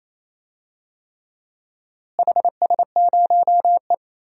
Morse code, heard as a keyed tone, about 28 wpm, 700 Hz. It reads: HS0E
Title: DXCC-100 Call Signs - 4 Characters - Repeated 3x - 28wpm